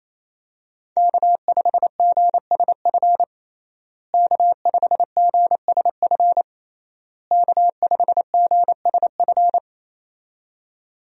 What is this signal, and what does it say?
Morse code, heard as a keyed tone, about 27 wpm, 700 Hz. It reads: K5GSF K5GSF K5GSF
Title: USA Call Signs - 5 Characters - Repeated 3x - 28wpm